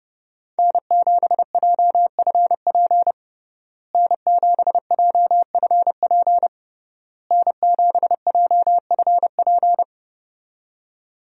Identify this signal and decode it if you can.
Morse code, heard as a keyed tone, about 31 wpm, 700 Hz. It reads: N7JFP N7JFP N7JFP
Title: USA Call Signs - 5 Characters - Repeated 3x - 30wpm